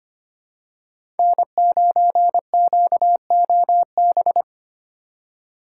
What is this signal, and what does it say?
Morse code, heard as a keyed tone, about 25 wpm, 700 Hz. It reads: N9QOB
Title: USA Call Signs - 5 Characters - 4x Character Spacing - 25wpm